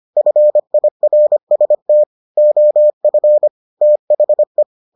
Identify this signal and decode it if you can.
Morse code, heard as a keyed tone, about 25 wpm, 600 Hz. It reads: FIRST OF THE